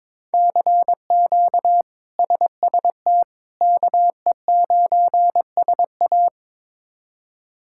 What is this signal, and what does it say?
Morse code, heard as a keyed tone, about 22 wpm, 700 Hz. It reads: CQ SST KE9SA
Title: K1USN SST QSO Practice - 22wpm